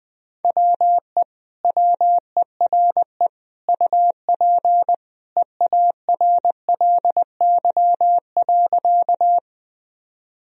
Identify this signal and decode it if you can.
Morse code, heard as a keyed tone, about 21 wpm, 700 Hz. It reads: WE WERE UP EARLY.
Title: Sentences from Top 1000 Words - Review Part 2 - 20wpm